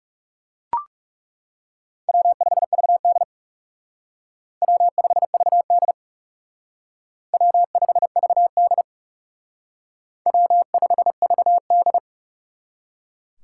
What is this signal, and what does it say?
Morse code, heard as a keyed tone, about 35 wpm, 700 Hz. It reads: W5VD W5VD W5VD W5VD
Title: Speed Racing - USA Call Signs 30wpm